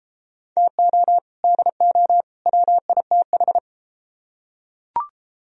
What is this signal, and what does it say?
Morse code, heard as a keyed tone, about 33 wpm, 700 Hz. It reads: TO DO WITH